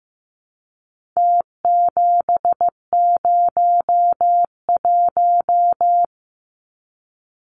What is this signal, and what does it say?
Morse code, heard as a keyed tone, about 15 wpm, 700 Hz. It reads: T701
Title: Numbers - Contest Style - 4-Digit 15wpm